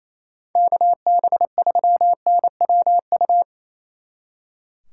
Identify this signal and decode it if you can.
Morse code, heard as a keyed tone, about 35 wpm, 700 Hz. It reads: KB3NWU
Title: Speed Racing - USA 2x3 Call Signs 28wpm